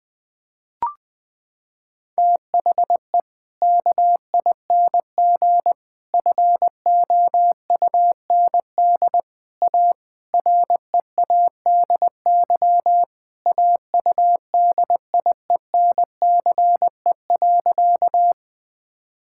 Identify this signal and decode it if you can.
Morse code, heard as a keyed tone, about 20 wpm, 700 Hz. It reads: THE KING FOUND A READY AUDIENCE.
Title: Sentences from Top 1000 Words - Review Part 5 - 20wpm